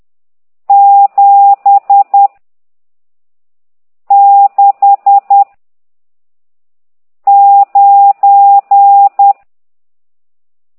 Morse code, heard as a keyed tone, about 10 wpm, 800 Hz. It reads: 7 6 9
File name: ct2idw / exerc14